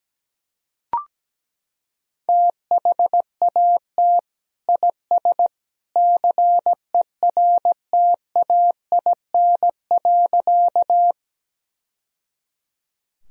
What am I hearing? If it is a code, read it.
Morse code, heard as a keyed tone, about 17 wpm, 700 Hz. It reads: THAT IS CERTAIN.
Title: Sentences from Top 1000 Words - Review Part 2 - 17wpm